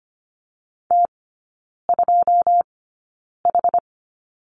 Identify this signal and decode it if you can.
Morse code, heard as a keyed tone, about 25 wpm, spaced as for 9 wpm, 700 Hz. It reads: T2H
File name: course / 25:10wpm / 093 Lesson 16e - Letter c - Three Characters 25wpm